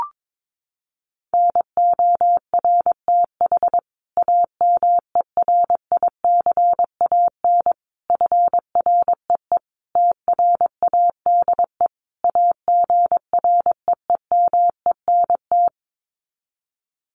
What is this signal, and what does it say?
Morse code, heard as a keyed tone, about 22 wpm, 700 Hz. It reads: NORTH AMERICAN FREE TRADE AGREEMENT
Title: Sets of 5 Words Encore 22wpm